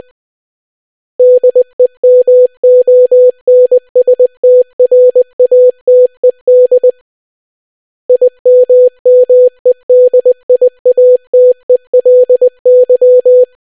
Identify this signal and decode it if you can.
Morse code, heard as a keyed tone, about 20 wpm, 500 Hz. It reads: DEMONSTRATED IMMEDIATELY